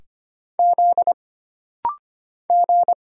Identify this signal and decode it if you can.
Morse code, heard as a keyed tone, about 25 wpm, 700 Hz. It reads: Z G
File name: Single Character (Letter:Number:Punctuation:Prosign) - Rapid-Fire - 25wpm